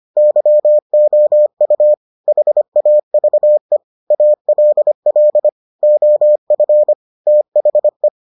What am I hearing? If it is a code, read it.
Morse code, heard as a keyed tone, about 25 wpm, 600 Hz. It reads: YOU HAVE ALL OF THE